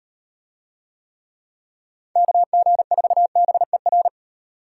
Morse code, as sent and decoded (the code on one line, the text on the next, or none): -.- --. ....- -... .-.
KG4BER